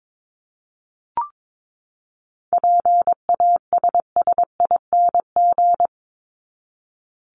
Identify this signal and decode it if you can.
Morse code, heard as a keyed tone, about 22 wpm, 700 Hz. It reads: PASSING
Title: ING Suffix - Top 100 Words with Base Words 22wpm